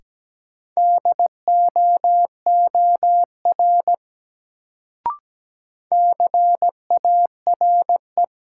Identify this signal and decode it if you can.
Morse code, heard as a keyed tone, about 17 wpm, 700 Hz. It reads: DOOR CARE
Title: Sentences from Top 300 Words - New Word Drill - 17wpm